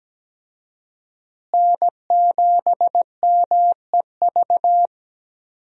Morse code, heard as a keyed tone, about 17 wpm, 700 Hz. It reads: N7MEV
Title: USA Call Signs - 5 Characters - Repeated 2x - 17wpm